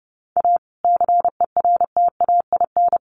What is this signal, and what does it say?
Morse code, heard as a keyed tone, about 30 wpm, 700 Hz. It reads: A CERTAIN
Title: Sentences from Top 1300 Words 30wpm